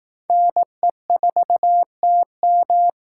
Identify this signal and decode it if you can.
Morse code, heard as a keyed tone, about 18 wpm, 700 Hz. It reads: NE4TM
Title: Speed Racing - USA 2x2 Call Signs 15wpm